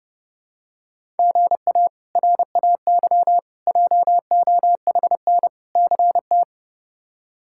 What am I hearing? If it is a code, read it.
Morse code, heard as a keyed tone, about 30 wpm, 700 Hz. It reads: GA RAY JOHN CT
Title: K1USN SST QSO Practice - 30wpm